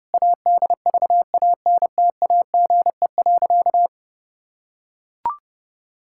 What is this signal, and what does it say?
Morse code, heard as a keyed tone, about 30 wpm, 700 Hz. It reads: ADVANTAGE.
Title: Sentences from Top 1000 Words - Review Part 6 - 30wpm